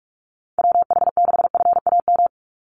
Morse code, heard as a keyed tone, about 44 wpm, 700 Hz. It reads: WH6FAM
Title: USA Call Signs 45wpm